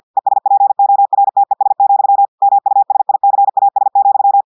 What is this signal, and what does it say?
Morse code, heard as a keyed tone, about 50 wpm, 800 Hz. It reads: ESPORTES, MUSICAS,
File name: portugais texte en clair 2 vitesse 50 mots minutes